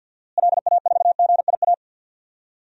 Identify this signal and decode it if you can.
Morse code, heard as a keyed tone, about 51 wpm, 700 Hz. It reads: LA4GIA
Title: DXCC-100 Call Signs - 6 Characters - Repeated 2x - 50wpm